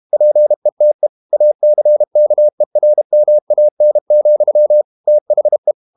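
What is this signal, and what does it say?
Morse code, heard as a keyed tone, about 32 wpm, 600 Hz. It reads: PETE ACKERMAN, THE
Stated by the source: Morse code practice at 32 wpm from CW med Gnister, reading Ken Follett's novel Never in English.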